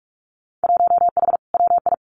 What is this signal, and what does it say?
Morse code, heard as a keyed tone, about 46 wpm, 700 Hz. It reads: 1H WI
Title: Winter Field Day Exchanges 45wpm